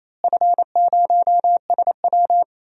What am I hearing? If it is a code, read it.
Morse code, heard as a keyed tone, about 27 wpm, 700 Hz. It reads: F0SW
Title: DXCC-100 Call Signs - 4 Characters - Repeated 3x - 28wpm